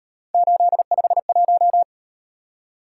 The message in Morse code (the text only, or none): 851